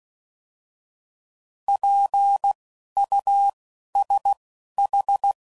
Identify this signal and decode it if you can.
Morse code, heard as a keyed tone, about 16 wpm, spaced as for 11 wpm, 800 Hz. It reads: PUSH
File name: Prefixes / By First Letter / Ps